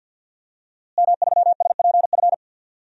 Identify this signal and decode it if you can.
Morse code, heard as a keyed tone, about 52 wpm, 700 Hz. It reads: M3SPF
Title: DXCC-100 Call Signs - 5 Characters - Repeated 3x - 50wpm